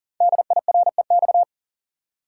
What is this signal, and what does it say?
Morse code, heard as a keyed tone, about 41 wpm, 690 Hz. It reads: DIREX
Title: Code Groups - 5 Letters - 40wpm